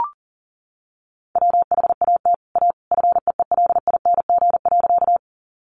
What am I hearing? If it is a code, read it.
Morse code, heard as a keyed tone, about 40 wpm, 700 Hz. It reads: WHAT A FEELING.
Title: Sentences from Top 1200 Words 40wpm